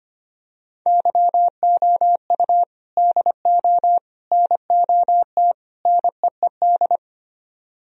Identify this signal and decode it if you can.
Morse code, heard as a keyed tone, about 24 wpm, 700 Hz. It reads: YOU DO NOT NEED
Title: Speed-Racing - Sets of 4 Words 25wpm